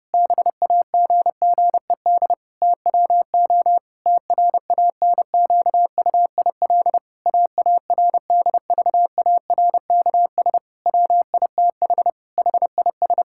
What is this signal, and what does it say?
Morse code, heard as a keyed tone, about 32 wpm, 700 Hz. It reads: BAGGED TWO TRANQUIL AARDVARKS WITH HIS
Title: Pangram Challenge v2 - 30wpm